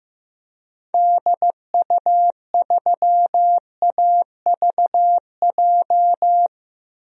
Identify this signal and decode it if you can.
Morse code, heard as a keyed tone, about 16 wpm, 700 Hz. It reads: DU3AVJ
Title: DXCC-100 Call Signs - 6 Characters - Repeated 2x - 15wpm